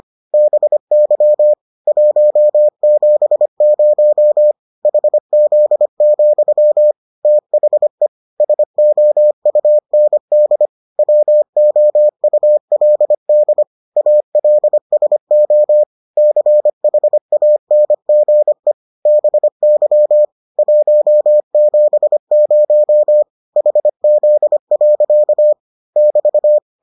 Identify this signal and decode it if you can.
Morse code, heard as a keyed tone, about 25 wpm, 600 Hz. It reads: BY 170 HZ, THE SOUND WOULD ALSO CHANGE BY 170 HZ. <BT>